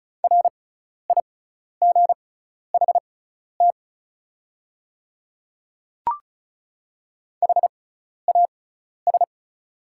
Morse code, heard as a keyed tone, about 34 wpm, spaced as for 12 wpm, 690 Hz. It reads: RIGHT HAS
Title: 6x Character Spacing - Top 500 Words - 35wpm